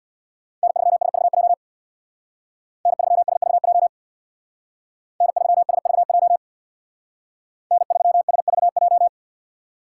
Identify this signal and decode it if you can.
Morse code, heard as a keyed tone, about 61 wpm, 700 Hz. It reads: N3SVJ N3SVJ N3SVJ N3SVJ
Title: Speed Racing - USA 1x3 Call Signs 50wpm